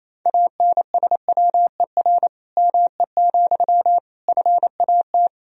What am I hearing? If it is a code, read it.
Morse code, heard as a keyed tone, about 28 wpm, 700 Hz. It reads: ANSWER ME, FAT
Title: Sentences from Top 1000 Words - Review Part 4 - 28wpm